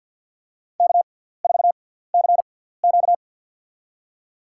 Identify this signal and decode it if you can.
Morse code, heard as a keyed tone, about 52 wpm, spaced as for 17 wpm, 700 Hz. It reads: K4CQ
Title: USA Call Signs - 6x Character Spacing - 50wpm